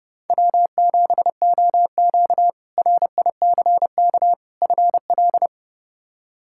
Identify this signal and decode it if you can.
Morse code, heard as a keyed tone, about 30 wpm, 700 Hz. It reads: W7OQ RICK FL
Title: K1USN SST QSO Practice - 30wpm